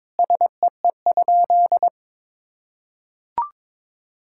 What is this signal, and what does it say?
Morse code, heard as a keyed tone, about 22 wpm, 700 Hz. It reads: SEE?
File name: practice / 22wpm / Sentences with 3-Word Phrases - 22wpm